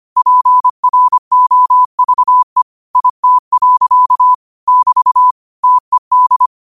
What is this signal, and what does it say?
Morse code, heard as a keyed tone, about 25 wpm, 1000 Hz. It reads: PROVE IT. <BT> TED
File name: qotdfiles / 25wpm